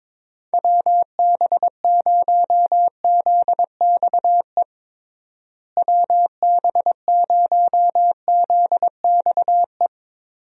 Morse code, heard as a keyed tone, about 22 wpm, 700 Hz. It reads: WB0ZXE WB0ZXE
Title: USA Call Signs - 6 Characters - Repeated 3x - 22wpm